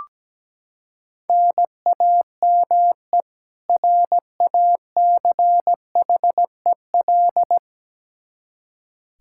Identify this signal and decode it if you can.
Morse code, heard as a keyed tone, about 17 wpm, 700 Hz. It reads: NAME RACHEL
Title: Practice QSO - Exchange 1 - Beginning - 17wpm